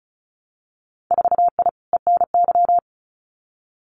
Morse code, heard as a keyed tone, about 36 wpm, 700 Hz. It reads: 4I ENY
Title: Winter Field Day Exchanges Encore 35wpm